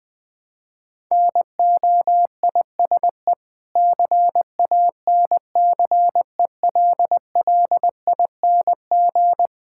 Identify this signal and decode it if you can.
Morse code, heard as a keyed tone, about 20 wpm, 690 Hz. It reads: NOISE CANCELLING
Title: Difficult Words with Hints - 20wpm